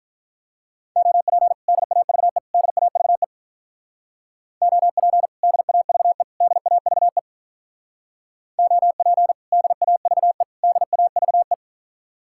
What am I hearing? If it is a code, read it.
Morse code, heard as a keyed tone, about 44 wpm, 700 Hz. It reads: OP DAVE DAVE OP DAVE DAVE OP DAVE DAVE